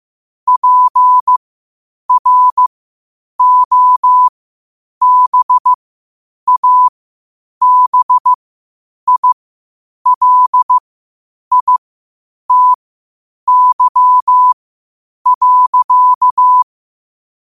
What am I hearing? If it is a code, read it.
Morse code, heard as a keyed tone, about 15 wpm, spaced as for 8 wpm, 1000 Hz. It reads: PROBABILITY.
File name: qotdfiles / 8wpm